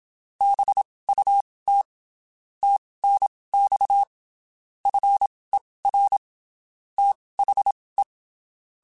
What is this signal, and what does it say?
Morse code, heard as a keyed tone, about 27 wpm, spaced as for 19 wpm, 800 Hz. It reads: BUT TNX FER THE
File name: (spaced) / QSOs Spaced-mp3s / kc7haa @25 sp